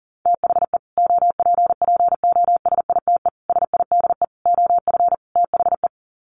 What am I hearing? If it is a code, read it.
Morse code, heard as a keyed tone, about 40 wpm, 700 Hz. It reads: THE OPPOSITE SIDE OF THE
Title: Sets of 5 Words Encore 40wpm